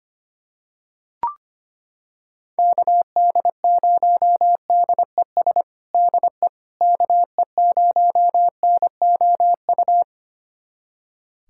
Morse code, heard as a keyed tone, about 25 wpm, 700 Hz. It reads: KD0DES DE KE0NOU
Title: USA Callsign DE Callsign - 6 Characters - 25wpm